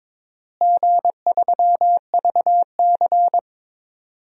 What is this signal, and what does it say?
Morse code, heard as a keyed tone, about 22 wpm, 700 Hz. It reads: G3VC
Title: DXCC-100 Call Signs - 4 Characters - Repeated 2x - 22wpm